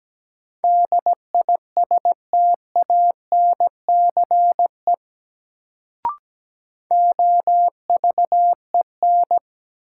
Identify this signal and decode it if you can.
Morse code, heard as a keyed tone, about 17 wpm, 700 Hz. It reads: DISTANCE OVEN
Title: Sentences from Top 1200 Words - New Word Drill Alt - 17wpm